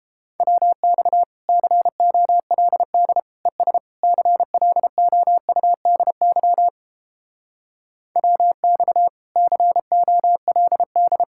Morse code, heard as a keyed tone, about 29 wpm, 700 Hz. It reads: WX COLD ES CLOUDY WX COLD